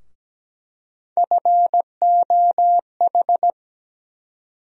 Morse code, heard as a keyed tone, about 17 wpm, 700 Hz. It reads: FOH
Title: USA Call Sign Postfixes - 3 Letters 17wpm